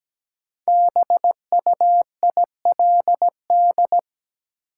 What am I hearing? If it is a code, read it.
Morse code, heard as a keyed tone, about 17 wpm, 700 Hz. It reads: BUILD
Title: Sentences from Top 1000 Words - Review Part 1 - New Word Drill - 17wpm